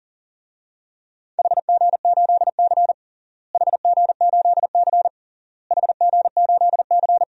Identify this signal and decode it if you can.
Morse code, heard as a keyed tone, about 40 wpm, 700 Hz. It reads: HG8C HG8C HG8C
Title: DXCC-100 Call Signs - 4 Characters - Repeated 3x - 40wpm